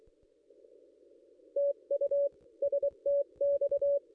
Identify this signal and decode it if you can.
Morse code, heard as a keyed tone, about 24 wpm, 570 Hz. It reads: TU STX
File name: contests / ss-cw-k5tr-op-n5rz -1900